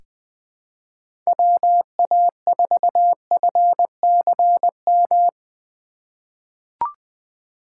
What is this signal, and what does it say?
Morse code, heard as a keyed tone, about 20 wpm, 700 Hz. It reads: WA4FCM
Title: USA Call Signs - 2x Character Spacing - 20wpm